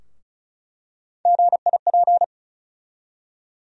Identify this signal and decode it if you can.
Morse code, heard as a keyed tone, about 36 wpm, 700 Hz. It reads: GIP
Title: USA Call Sign Postfixes - 3 Letters 35wpm